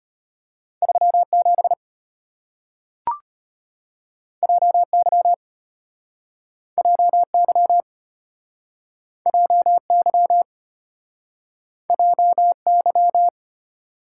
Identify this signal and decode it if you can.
Morse code, heard as a keyed tone, about 34 wpm, 700 Hz. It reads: 37 JY JY JY JY